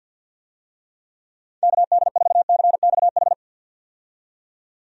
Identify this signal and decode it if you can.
Morse code, heard as a keyed tone, about 50 wpm, 700 Hz. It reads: KD4CXH
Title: USA Call Signs - 6 Characters - 50wpm